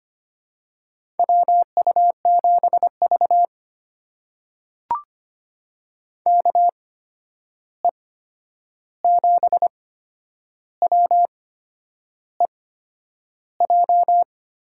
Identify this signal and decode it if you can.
Morse code, heard as a keyed tone, about 25 wpm, spaced as for 9 wpm, 700 Hz. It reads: WU7V KE7WEJ
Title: USA Call Signs - 8x Character Spacing - 25wpm